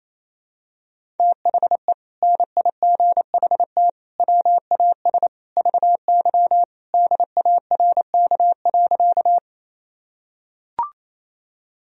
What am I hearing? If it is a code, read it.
Morse code, heard as a keyed tone, about 28 wpm, 700 Hz. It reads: THE NIGHT WAS VY DARK.
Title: Sentences from Top 500 Words and Abbreviations 28wpm